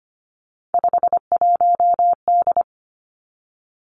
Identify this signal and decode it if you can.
Morse code, heard as a keyed tone, about 25 wpm, 700 Hz. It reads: 51D
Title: Three Challenging Letter-Number Combinations 25wpm